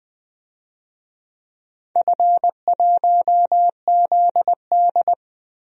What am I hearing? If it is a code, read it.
Morse code, heard as a keyed tone, about 20 wpm, 700 Hz. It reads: F1ZD
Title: DXCC-100 Call Signs - 4 Characters - Repeated 2x - 20wpm